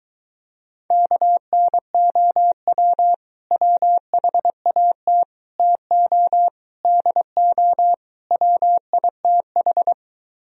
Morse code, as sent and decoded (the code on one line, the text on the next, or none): -.- -. --- .-- / .-- .... .- - / - --- / -.. --- / .-- .. - ....
KNOW WHAT TO DO WITH